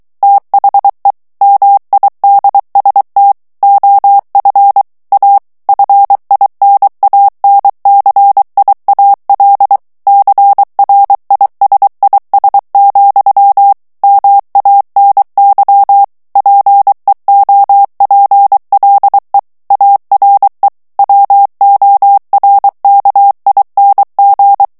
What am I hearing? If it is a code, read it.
Morse code, heard as a keyed tone, about 23 wpm, 800 Hz. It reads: THE MIDST OF A FINANCIAL CRISIS, MANY PEOPLE ARE WORKING
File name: morse mp3 / ANGLAIS texte clair 1 vitesse 22 mots minute